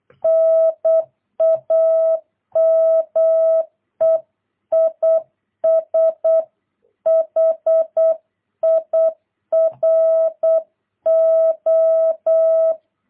Morse code, heard as a keyed tone, about 8 wpm, 650 Hz. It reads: NAME IS HIRO
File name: jh8cbh / 2022cw